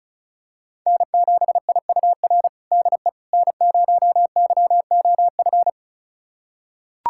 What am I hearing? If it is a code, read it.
Morse code, heard as a keyed tone, about 34 wpm, 690 Hz. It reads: N7IUR DE N0YOF
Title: USA Callsign DE Callsign - 5 Characters - 35wpm